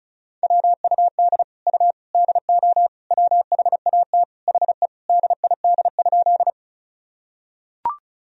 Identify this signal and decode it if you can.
Morse code, heard as a keyed tone, about 36 wpm, 700 Hz. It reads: WUD U DO WHAT HE DID?
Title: Sentences from Top 100 Words and Abbreviations 35wpm